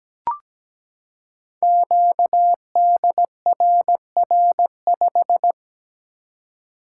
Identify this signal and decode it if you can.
Morse code, heard as a keyed tone, about 17 wpm, 700 Hz. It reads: QDRR5
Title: Code Groups - 5 Letter:Numbers - 17wpm